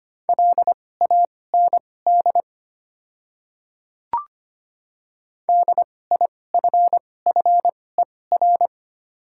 Morse code, heard as a keyed tone, about 25 wpm, spaced as for 18 wpm, 690 Hz. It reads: LAND DIFFER
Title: 2x Character Spacing - Top 500 Words - 25wpm